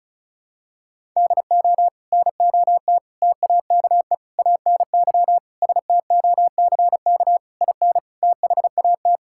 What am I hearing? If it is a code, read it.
Morse code, heard as a keyed tone, about 36 wpm, 700 Hz. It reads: DO NOT TAKE ANY STOCK IN THAT
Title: Sentences from Top 1000 Words - Review Part 5 - 35wpm